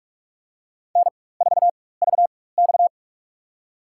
Morse code, heard as a keyed tone, about 46 wpm, spaced as for 21 wpm, 700 Hz. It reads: N4VX
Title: USA Call Signs - 4 Characters - 4x Character Spacing - 45wpm